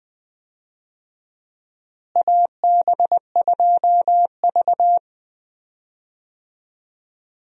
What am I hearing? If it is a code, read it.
Morse code, heard as a keyed tone, about 20 wpm, 700 Hz. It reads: AB2V
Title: USA Call Signs - 4 Characters - Repeated 2x - 20wpm